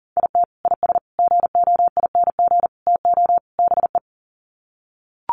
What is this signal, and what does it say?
Morse code, heard as a keyed tone, about 40 wpm, 700 Hz. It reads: IT IS GOING TO BE
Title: Sets of 5 Words 40wpm